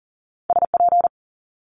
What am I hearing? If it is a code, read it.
Morse code, heard as a keyed tone, about 42 wpm, 700 Hz. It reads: SP
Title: Multi-letter Phonemes Quick Fire 40wpm